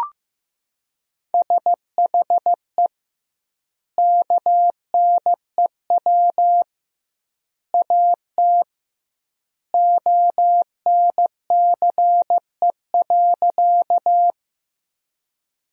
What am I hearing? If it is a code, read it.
Morse code, heard as a keyed tone, about 15 wpm, 700 Hz. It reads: SHE KNEW AT ONCE.
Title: 2x Word-Spacing - Sentences from Top 400 Words - 15wpm